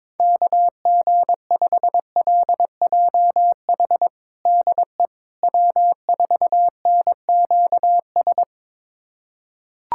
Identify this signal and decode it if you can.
Morse code, heard as a keyed tone, about 22 wpm, 700 Hz. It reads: KG5LJH DE W4NQS
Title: USA Callsign DE Callsign 22wpm